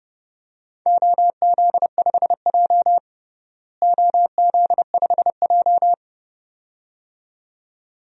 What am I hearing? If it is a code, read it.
Morse code, heard as a keyed tone, about 31 wpm, 700 Hz. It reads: OZ5J OZ5J
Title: DXCC-100 Call Signs - 4 Characters - Repeated 2x - 30wpm